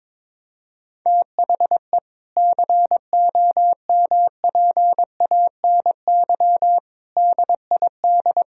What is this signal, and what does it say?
Morse code, heard as a keyed tone, about 22 wpm, 700 Hz. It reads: THE COMPANY DID
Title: Sentences from Top 1000 Words - Review Part 3 - 22wpm